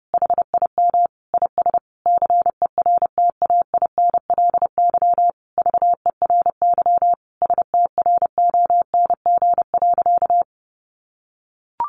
Morse code, heard as a keyed tone, about 30 wpm, 700 Hz. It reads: HIM IS CERTAINLY VERY STRONG.
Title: Sentences from Top 1300 Words 30wpm